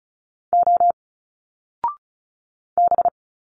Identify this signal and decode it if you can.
Morse code, heard as a keyed tone, about 36 wpm, 700 Hz. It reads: O B
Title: Single Letters 35wpm